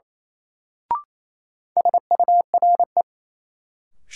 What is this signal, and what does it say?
Morse code, heard as a keyed tone, about 28 wpm, 700 Hz. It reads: SURE
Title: Sentences from Top 300 Words - New Word Drill - 28wpm